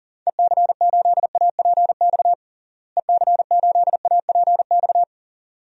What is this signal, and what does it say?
Morse code, heard as a keyed tone, about 40 wpm, 700 Hz. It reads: EC8APX EC8APX
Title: DXCC-100 Call Signs - 6 Characters - Repeated 3x - 40wpm